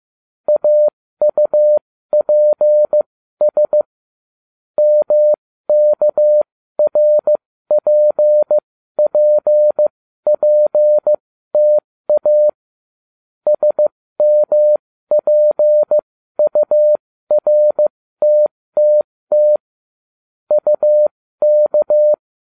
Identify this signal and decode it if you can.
Morse code, heard as a keyed tone, about 15 wpm, spaced as for 13 wpm, 600 Hz. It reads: AUPS MKRPPPTA SMPURTTT UK